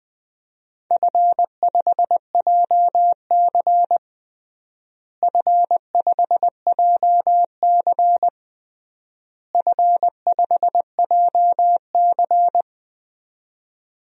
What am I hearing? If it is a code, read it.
Morse code, heard as a keyed tone, about 20 wpm, 700 Hz. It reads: F5JC F5JC F5JC
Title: DXCC-100 Call Signs - 4 Characters - Repeated 3x - 20wpm